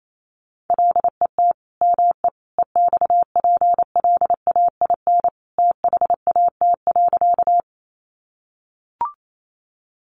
Morse code, heard as a keyed tone, about 28 wpm, 700 Hz. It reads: LET ME EXPLAIN THAT.
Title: Sentences from Top 1100 Words 28wpm